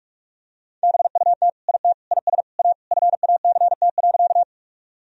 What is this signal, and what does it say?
Morse code, heard as a keyed tone, about 41 wpm, 700 Hz. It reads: BUT IT IS A FACT.
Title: Sentences from Top 1000 Words - Review Part 3 - 45wpm